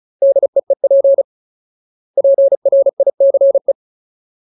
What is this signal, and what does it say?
Morse code, heard as a keyed tone, about 35 wpm, 550 Hz. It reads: DEEP PRICE
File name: Ultimate Word List / 35WPM Ultimate